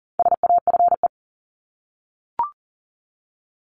Morse code, heard as a keyed tone, about 40 wpm, 700 Hz. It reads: SAFE
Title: Top 751-1000 Words 40wpm